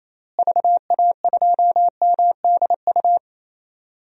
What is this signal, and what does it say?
Morse code, heard as a keyed tone, about 28 wpm, 700 Hz. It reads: VA2MDU
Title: DXCC-100 Call Signs - 6 Characters - Repeated 3x - 28wpm